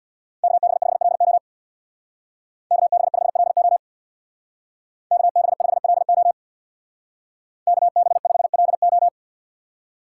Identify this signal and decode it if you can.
Morse code, heard as a keyed tone, about 61 wpm, 700 Hz. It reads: KB5LO KB5LO KB5LO KB5LO
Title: Speed Racing - USA Call Signs 50wpm